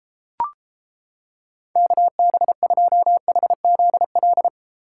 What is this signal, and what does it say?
Morse code, heard as a keyed tone, about 31 wpm, 700 Hz. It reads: KB2HZL